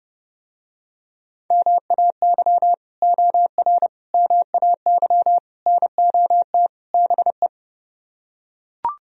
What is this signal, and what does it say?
Morse code, heard as a keyed tone, about 30 wpm, 700 Hz. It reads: MAY OR MAY NOT BE